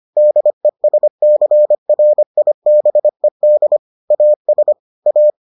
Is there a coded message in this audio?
Yes: Morse code from 0.2 to 5.3 s, heard as a keyed tone, about 25 wpm, 600 Hz. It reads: DESCRIBED AS A